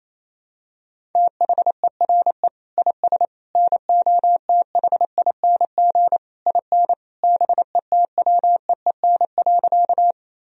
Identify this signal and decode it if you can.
Morse code, heard as a keyed tone, about 27 wpm, 700 Hz. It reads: THERE IS NOTHING IN BETWEEN.
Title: Sentences from Top 1000 Words - Review Part 1 - 28wpm